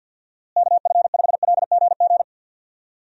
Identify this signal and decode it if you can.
Morse code, heard as a keyed tone, about 50 wpm, 700 Hz. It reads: KF5LGG
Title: USA Call Signs - 6 Characters - Repeated 2x - 50wpm